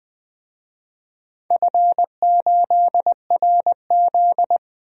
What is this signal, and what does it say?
Morse code, heard as a keyed tone, about 20 wpm, 700 Hz. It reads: F8RZ
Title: DXCC-100 Call Signs - 4 Characters - 20wpm